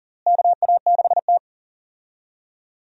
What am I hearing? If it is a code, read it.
Morse code, heard as a keyed tone, about 40 wpm, 700 Hz. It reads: KA6T
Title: USA Call Signs - 4 Characters - Repeated 2x - 40wpm